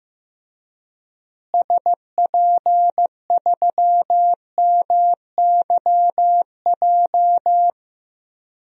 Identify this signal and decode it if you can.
Morse code, heard as a keyed tone, about 15 wpm, 700 Hz. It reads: SP3MYJ
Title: DXCC-100 Call Signs - 6 Characters - Repeated 3x - 15wpm